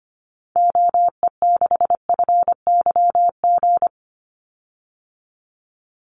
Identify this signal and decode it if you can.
Morse code, heard as a keyed tone, about 25 wpm, 700 Hz. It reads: OE6FYG
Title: Call Signs Repeated 3x - 25wpm